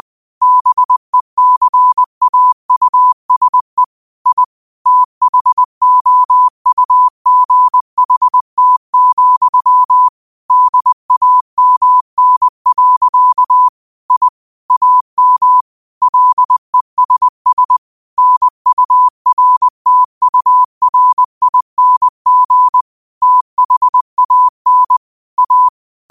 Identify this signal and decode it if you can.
Morse code, heard as a keyed tone, about 20 wpm, 1000 Hz. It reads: BECAUSE I THOUGHT, DAMN. I AM LESS NURTURING THAN A
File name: qotdfiles / 20wpm